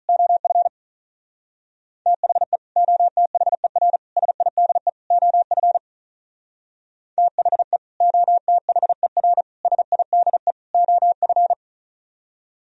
Morse code, heard as a keyed tone, about 44 wpm, 700 Hz. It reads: OF THE OTHER SIDE OF THE OTHER SIDE OF